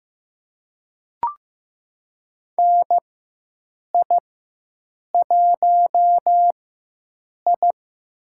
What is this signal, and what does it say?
Morse code, heard as a keyed tone, about 15 wpm, spaced as for 7 wpm, 700 Hz. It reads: NI1I
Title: USA Call Signs - 4 Characters - 4x Character Spacing - 15wpm